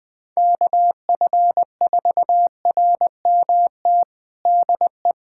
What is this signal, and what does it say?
Morse code, heard as a keyed tone, about 20 wpm, 700 Hz. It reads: KF4RMT DE
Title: USA Callsign DE Callsign - 6 Characters - 20wpm